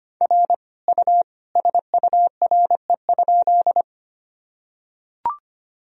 Morse code, heard as a keyed tone, about 25 wpm, 700 Hz. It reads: R U SURE?
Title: Sentences from Top 300 Words and Abbreviations 25wpm